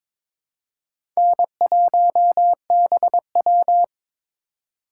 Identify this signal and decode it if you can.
Morse code, heard as a keyed tone, about 22 wpm, 700 Hz. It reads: N1BW
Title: USA Call Signs - 2x Character Spacing - 22wpm